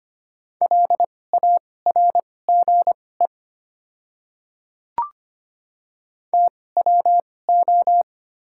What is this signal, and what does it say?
Morse code, heard as a keyed tone, about 25 wpm, spaced as for 18 wpm, 700 Hz. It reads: LARGE TWO
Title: 2x Character Spacing - Top 500 Words - 25wpm